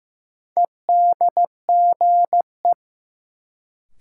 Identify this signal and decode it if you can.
Morse code, heard as a keyed tone, about 15 wpm, 700 Hz. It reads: EDGE
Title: Sentences from Top 1300 Words - New Word Drill - 15wpm